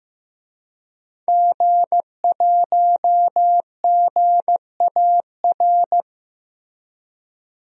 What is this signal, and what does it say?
Morse code, heard as a keyed tone, about 15 wpm, 700 Hz. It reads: G1GAR
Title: DXCC-100 Call Signs - 5 Characters - Repeated 2x - 15wpm